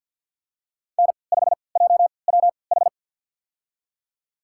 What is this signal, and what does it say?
Morse code, heard as a keyed tone, about 49 wpm, spaced as for 28 wpm, 690 Hz. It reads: N5JWH
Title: USA Call Signs - 5 Characters - 3x Character Spacing - 50wpm